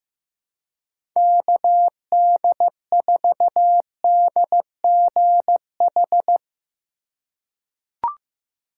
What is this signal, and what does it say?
Morse code, heard as a keyed tone, about 15 wpm, 700 Hz. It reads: KD4DGH
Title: USA Call Signs - 6 Characters - 3x Character Spacing - 15wpm